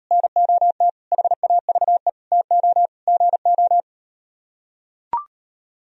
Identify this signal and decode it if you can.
Morse code, heard as a keyed tone, about 38 wpm, 700 Hz. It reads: NOT HAVE TO GO